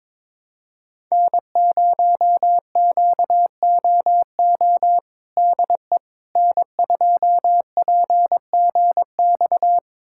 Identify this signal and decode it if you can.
Morse code, heard as a keyed tone, about 22 wpm, 700 Hz. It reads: N0QOO DE N2PGX
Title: USA Callsign DE Callsign - 5 Characters - 22wpm